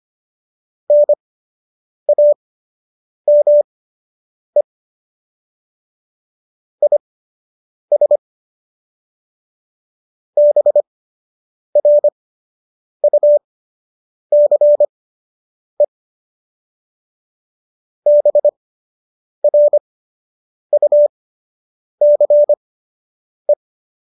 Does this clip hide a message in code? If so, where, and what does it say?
Yes: Morse code from 0.9 to 23.5 s, heard as a keyed tone, about 25 wpm, spaced as for 8 wpm, 590 Hz. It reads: NAME IS BRUCE BRUCE